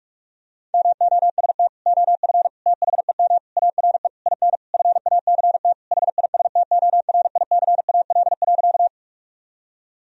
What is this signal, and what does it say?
Morse code, heard as a keyed tone, about 46 wpm, 700 Hz. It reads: MOST OF THEM ARE IN FACT HISTORICAL.
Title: Sentences from Top 1000 Words - Review Part 4 - 45wpm